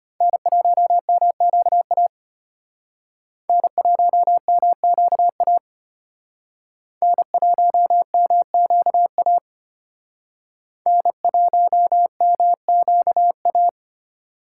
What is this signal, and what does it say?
Morse code, heard as a keyed tone, about 32 wpm, 700 Hz. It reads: N1MQA N1MQA N1MQA N1MQA